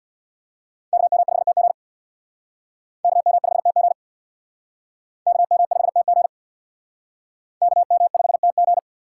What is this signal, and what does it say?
Morse code, heard as a keyed tone, about 64 wpm, 700 Hz. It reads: KM5TG KM5TG KM5TG KM5TG